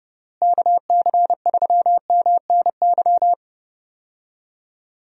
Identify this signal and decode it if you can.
Morse code, heard as a keyed tone, about 30 wpm, 700 Hz. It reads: KC3MNY
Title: USA Call Signs - 6 Characters - Repeated 2x - 30wpm